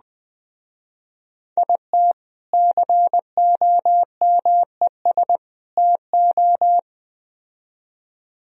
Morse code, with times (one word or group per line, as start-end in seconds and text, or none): 1.6-2.1 IT
2.5-5.4 COMES
5.8-6.8 TO